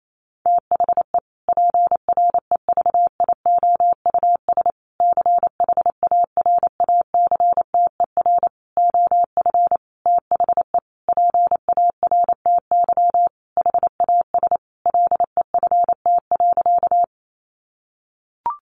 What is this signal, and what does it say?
Morse code, heard as a keyed tone, about 28 wpm, 700 Hz. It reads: THE PREVIOUS CHARACTER OF THE PARTY HAS LEFT.
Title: Sentences from Top 1500 Words 28wpm